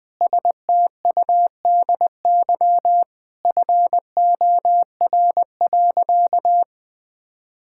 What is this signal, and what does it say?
Morse code, heard as a keyed tone, about 21 wpm, 700 Hz. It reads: STUDY FOR.
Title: Sentences from Top 1000 Words - Review Part 5 - 20wpm